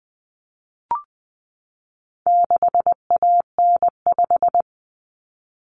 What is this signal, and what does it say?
Morse code, heard as a keyed tone, about 20 wpm, 700 Hz. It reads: 6AN5
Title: Common Cut Numbers - 4-Digit 20wpm